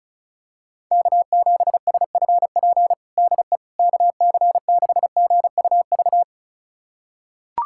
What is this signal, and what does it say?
Morse code, heard as a keyed tone, about 36 wpm, 700 Hz. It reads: K7SFP DE KC6GUV
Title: USA Callsign DE Callsign 35wpm